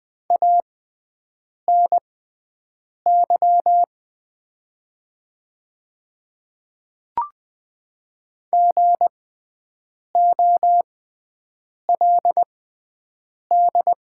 Morse code, heard as a keyed tone, about 20 wpm, spaced as for 7 wpm, 700 Hz. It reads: ANY GOLD
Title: 6x Character Spacing - Top 500 Words - 20wpm